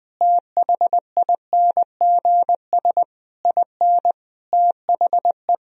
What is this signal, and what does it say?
Morse code, heard as a keyed tone, about 20 wpm, 700 Hz. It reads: THINGS IN THE
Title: Sentences from Top 1000 Words - Review Part 4 - 20wpm